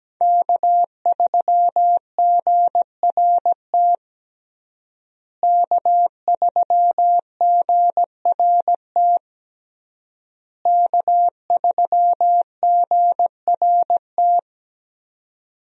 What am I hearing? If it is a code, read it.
Morse code, heard as a keyed tone, about 17 wpm, 700 Hz. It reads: K3GRT K3GRT K3GRT
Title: USA Call Signs - 5 Characters - Repeated 3x - 17wpm